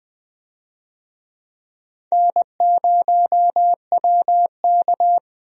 Morse code, heard as a keyed tone, about 20 wpm, 700 Hz. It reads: N0WK
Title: USA Call Signs - 4 Characters - Repeated 2x - 20wpm